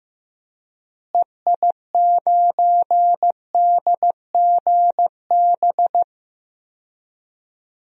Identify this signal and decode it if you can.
Morse code, heard as a keyed tone, about 16 wpm, 700 Hz. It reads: EI9DGB
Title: DXCC-100 Call Signs - 6 Characters - Repeated 2x - 15wpm